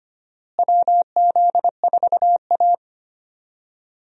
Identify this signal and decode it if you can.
Morse code, heard as a keyed tone, about 25 wpm, 700 Hz. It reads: WZ4A
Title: USA Call Signs - 4 Characters - Repeated 3x - 25wpm